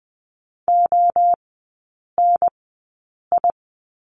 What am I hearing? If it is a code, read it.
Morse code, heard as a keyed tone, about 20 wpm, spaced as for 8 wpm, 700 Hz. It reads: ONI
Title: 041 Lesson 8e - Number 1 - Three Characters 20wpm